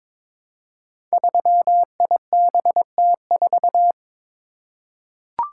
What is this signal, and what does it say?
Morse code, heard as a keyed tone, about 22 wpm, 700 Hz. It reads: 3IBT4
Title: Code Groups - 5 Letter:Numbers - 22wpm